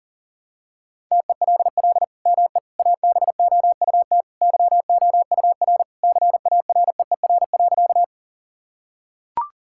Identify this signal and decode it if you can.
Morse code, heard as a keyed tone, about 40 wpm, 700 Hz. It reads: TELL ME ABOUT YOUR CAREER.
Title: Sentences from Top 1000 Words - Review Part 6 - 40wpm